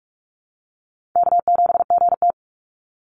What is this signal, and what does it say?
Morse code, heard as a keyed tone, about 46 wpm, 700 Hz. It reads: K7GT
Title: NA SOTA Chaser Callsigns 45wpm